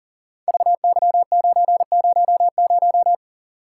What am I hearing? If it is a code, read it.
Morse code, heard as a keyed tone, about 38 wpm, 690 Hz. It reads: VY900
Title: Code Groups - 5 Letter:Numbers - 40wpm